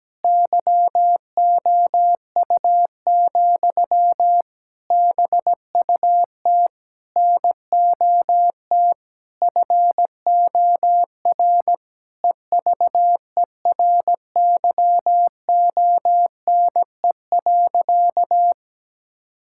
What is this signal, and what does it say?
Morse code, heard as a keyed tone, about 17 wpm, 700 Hz. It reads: YOU, BUT NOT FOR EVERYONE.
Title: Sentences from Top 1000 Words - Review Part 5 - 17wpm